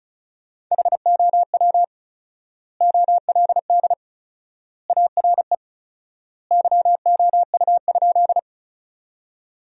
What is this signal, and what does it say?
Morse code, heard as a keyed tone, about 34 wpm, 700 Hz. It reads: HOW OLD ARE YOU?
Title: 4x Word-Spacing - Sentences from Top 200 Words - 35wpm